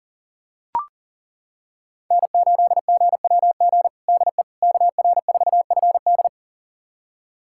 Morse code, heard as a keyed tone, about 40 wpm, 700 Hz. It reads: N8GWG DE KR4FD
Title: USA Callsign DE Callsign - 5 Characters - 40wpm